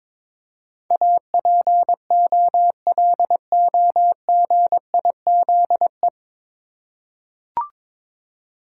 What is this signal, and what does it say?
Morse code, heard as a keyed tone, about 22 wpm, 700 Hz. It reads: APOLOGIZE
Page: Morse Code Ninja practice 22wpm